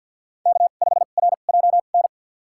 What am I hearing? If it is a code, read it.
Morse code, heard as a keyed tone, about 51 wpm, spaced as for 36 wpm, 700 Hz. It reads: K5RJN
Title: USA Call Signs - 5 Characters - 2x Character Spacing - 50wpm